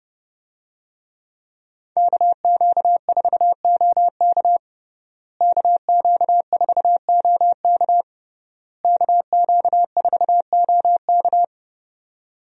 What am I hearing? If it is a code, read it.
Morse code, heard as a keyed tone, about 31 wpm, 700 Hz. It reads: KQ4OK KQ4OK KQ4OK
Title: USA Call Signs - 5 Characters - Repeated 3x - 30wpm